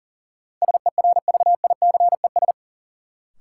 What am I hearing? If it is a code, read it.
Morse code, heard as a keyed tone, about 40 wpm, 700 Hz. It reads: SERVICES
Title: Sets of 5 Words - New Word Drill - 40wpm